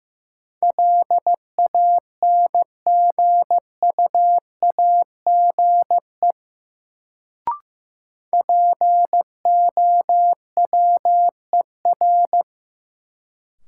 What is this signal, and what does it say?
Morse code, heard as a keyed tone, about 16 wpm, 700 Hz. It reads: LANGUAGE POWER
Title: Sentences from Top 500 Words - New Word Drill - 15wpm